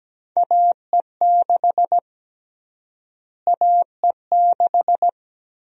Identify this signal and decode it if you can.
Morse code, heard as a keyed tone, about 17 wpm, 700 Hz. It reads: AE6 AE6
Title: USA Call Sign Prefixes - Repeated 2x - 17wpm